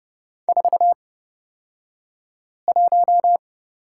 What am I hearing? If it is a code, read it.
Morse code, heard as a keyed tone, about 30 wpm, spaced as for 5 wpm, 700 Hz. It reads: 41
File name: Lesson 38d - Letter q - Two Characters 30wpm